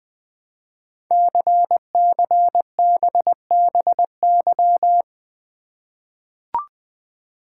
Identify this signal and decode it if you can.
Morse code, heard as a keyed tone, about 20 wpm, 700 Hz. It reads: CCBBY
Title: Code Groups - 5 Letters - 20wpm